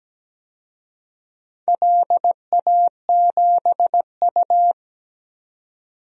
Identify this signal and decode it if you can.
Morse code, heard as a keyed tone, about 17 wpm, 700 Hz. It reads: LA7U
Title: DXCC-100 Call Signs - 4 Characters - 17wpm